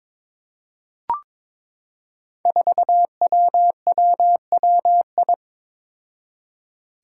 Morse code, heard as a keyed tone, about 22 wpm, 700 Hz. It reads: 4WWWI
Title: Five Challenging Letter-Number Combinations 22wpm